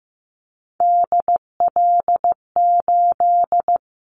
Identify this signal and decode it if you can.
Morse code, heard as a keyed tone, about 15 wpm, 700 Hz. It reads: DL8
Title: Callsign Prefixes for Top 25 DXCC - 15wpm